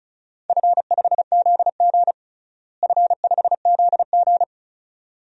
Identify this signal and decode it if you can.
Morse code, heard as a keyed tone, about 36 wpm, 700 Hz. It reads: F5ZG F5ZG
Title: DXCC-100 Call Signs - 4 Characters - Repeated 2x - 35wpm